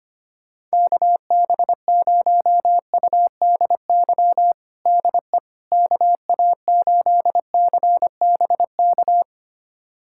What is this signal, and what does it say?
Morse code, heard as a keyed tone, about 25 wpm, 700 Hz. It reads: KB0UDY DE KA8CBK
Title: USA Callsign DE Callsign - 6 Characters - 25wpm